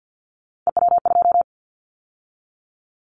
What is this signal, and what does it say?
Morse code, heard as a keyed tone, about 50 wpm, 690 Hz. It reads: EW2
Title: Callsign Prefixes for Top 100 DXCC - 50wpm